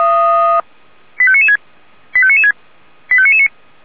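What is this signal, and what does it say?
Five-tone selective call (ZVEI), heard as quick runs of five tones, tones of about 70 ms. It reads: 85996 85996 85999
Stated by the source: ZVEI five-tone selective call received on an SDR.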